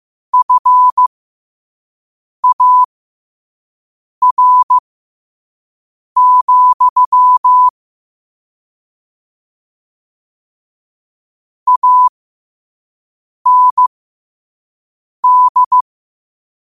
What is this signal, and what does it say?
Morse code, heard as a keyed tone, about 15 wpm, spaced as for 5 wpm, 1000 Hz. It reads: FAR, AND
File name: qotdfiles / 5wpm